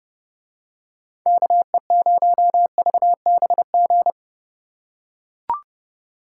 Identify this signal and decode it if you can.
Morse code, heard as a keyed tone, about 28 wpm, 700 Hz. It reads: KE0VBG